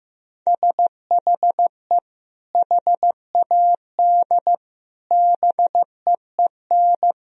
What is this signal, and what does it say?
Morse code, heard as a keyed tone, about 15 wpm, 700 Hz. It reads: SHE HAD BEEN
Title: Sentences from Top 1000 Words - Review Part 4 - 15wpm